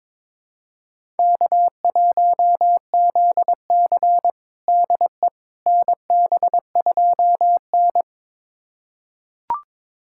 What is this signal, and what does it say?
Morse code, heard as a keyed tone, about 22 wpm, 700 Hz. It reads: K1ZC DE NB2N
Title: USA Callsign DE Callsign - 4 Characters - 22wpm